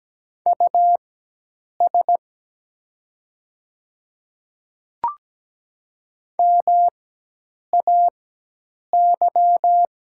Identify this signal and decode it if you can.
Morse code, heard as a keyed tone, about 17 wpm, spaced as for 8 wpm, 700 Hz. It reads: US MAY
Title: 4x Character Spacing - Top 200 Words - 17wpm